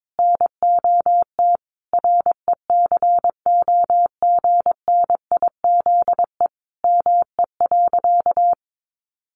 Morse code, heard as a keyed tone, about 22 wpm, 700 Hz. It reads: NOT RECOGNIZE ME.
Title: Sentences from Top 1400 Words 22wpm